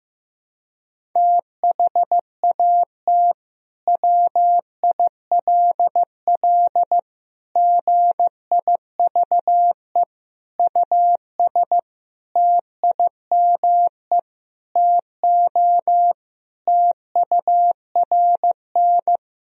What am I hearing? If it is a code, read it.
Morse code, heard as a keyed tone, about 15 wpm, 700 Hz. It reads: THAT WILL GIVE US TIME TO TURN
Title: Sentences from Top 1000 Words - Review Part 1 - 15wpm